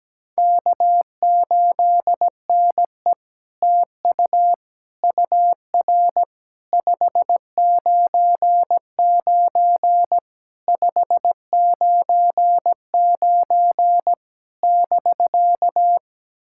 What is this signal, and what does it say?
Morse code, heard as a keyed tone, about 17 wpm, 700 Hz. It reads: K8NE TU UR 599 599 <BK>
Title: SOTA QSO Practice 17wpm